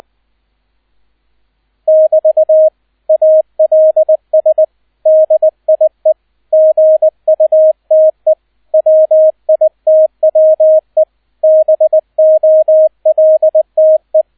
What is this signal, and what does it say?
Morse code, heard as a keyed tone, about 20 wpm, 630 Hz, with radio noise. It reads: <BT> ALS DIE GUTE WITWE BOLTE
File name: 002 Zweiter Streich 20wpm